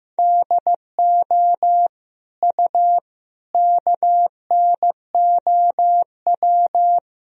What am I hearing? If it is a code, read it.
Morse code, heard as a keyed tone, about 15 wpm, 700 Hz. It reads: DO U KNOW
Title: Sentences from Top 100 Words and Abbreviations 15wpm